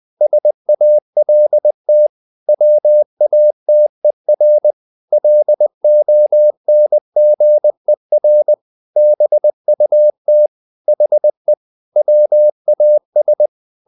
Morse code, heard as a keyed tone, about 20 wpm, 600 Hz. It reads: SALT WATER LONGER BUT HE WAS
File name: the old man and the sea / the old man and the sea part 1 20wpm